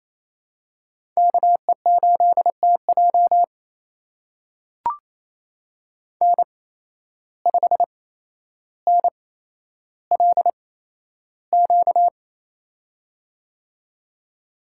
Morse code, heard as a keyed tone, about 28 wpm, spaced as for 12 wpm, 700 Hz. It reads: KE8TJ N5NLQ
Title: USA Call Signs - 5 Characters - 8x Character Spacing - 28wpm